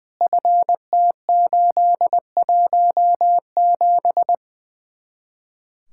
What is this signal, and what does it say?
Morse code, heard as a keyed tone, about 20 wpm, 700 Hz. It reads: FT817
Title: Speed-Racing Common QSO Phrases Encore - 20wpm